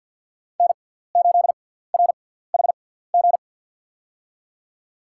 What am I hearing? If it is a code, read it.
Morse code, heard as a keyed tone, about 52 wpm, spaced as for 17 wpm, 700 Hz. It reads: N8RHG
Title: USA Call Signs - 5 Characters - 6x Character Spacing - 50wpm